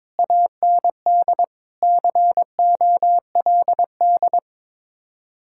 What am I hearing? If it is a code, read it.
Morse code, heard as a keyed tone, about 22 wpm, 700 Hz. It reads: AND COLD
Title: Speed-Racing Common QSO Phrases - 22wpm